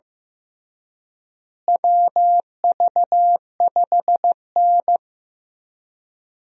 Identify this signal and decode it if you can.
Morse code, heard as a keyed tone, about 15 wpm, 700 Hz. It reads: WV5N
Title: USA Call Signs - 4 Characters - Repeated 3x - 15wpm